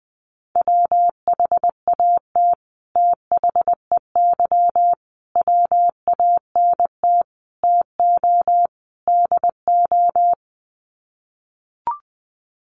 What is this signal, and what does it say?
Morse code, heard as a keyed tone, about 19 wpm, 700 Hz. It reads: WHAT THEY WANT TO DO
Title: Sets of 5 Words Encore2 20wpm